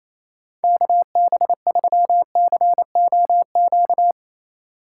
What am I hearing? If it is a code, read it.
Morse code, heard as a keyed tone, about 28 wpm, 700 Hz. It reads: KB3COQ
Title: USA Call Signs - 6 Characters - 8x Character Spacing - 28wpm